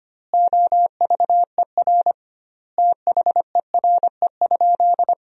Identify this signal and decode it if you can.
Morse code, heard as a keyed tone, about 25 wpm, 700 Hz. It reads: OVER THERE?
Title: 2x Word-Spacing - Sentences from Top 200 Words - 25wpm